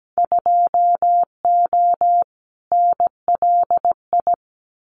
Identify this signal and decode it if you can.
Morse code, heard as a keyed tone, about 17 wpm, 700 Hz. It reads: 2O NLI
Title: Winter Field Day Exchanges 17wpm